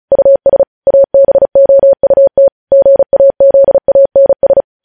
Morse code, heard as a keyed tone, about 36 wpm, 550 Hz. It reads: US ABOUT GAZANS
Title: News Headlines at 35